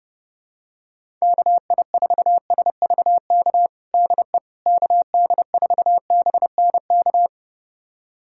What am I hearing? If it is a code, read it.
Morse code, heard as a keyed tone, about 30 wpm, 700 Hz. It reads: KI4SVK DE KD4BNK
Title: USA Callsign DE Callsign - 6 Characters - 30wpm